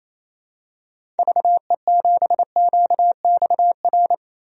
Morse code, heard as a keyed tone, about 28 wpm, 700 Hz. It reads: VE7QXR
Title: DXCC-100 Call Signs - 6 Characters - Repeated 2x - 28wpm